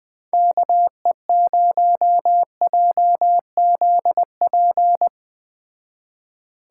Morse code, heard as a keyed tone, about 20 wpm, 700 Hz. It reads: KE0JZP
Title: Speed Racing - USA 2x3 Call Signs 17wpm